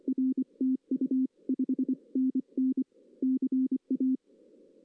Morse code, heard as a keyed tone, about 24 wpm, 260 Hz. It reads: R TU 5NN CA
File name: 2016 arrl dx cw / 2016-ARRL-DX-CW-K5TR- - 0-3